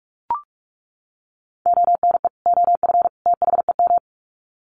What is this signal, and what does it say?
Morse code, heard as a keyed tone, about 44 wpm, 700 Hz. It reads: ONE OF THEM
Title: Sets of 3 Words 45wpm